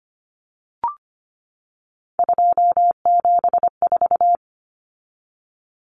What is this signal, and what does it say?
Morse code, heard as a keyed tone, about 25 wpm, 700 Hz. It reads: 274
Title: Numbers - 3-Digit 25wpm